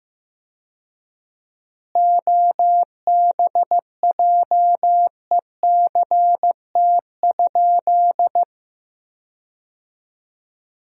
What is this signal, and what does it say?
Morse code, heard as a keyed tone, about 15 wpm, 700 Hz. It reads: OBJECT?
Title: 4x Word-Spacing - Sentences from Top 500 Words - 15wpm